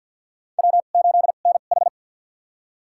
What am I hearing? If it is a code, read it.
Morse code, heard as a keyed tone, about 51 wpm, spaced as for 36 wpm, 700 Hz. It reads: W8NH
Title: USA Call Signs - 4 Characters - 2x Character Spacing - 50wpm